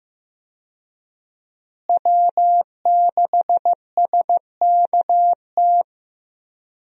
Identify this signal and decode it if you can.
Morse code, heard as a keyed tone, about 15 wpm, 700 Hz. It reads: W6SKT
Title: USA Call Signs - 5 Characters - Repeated 2x - 15wpm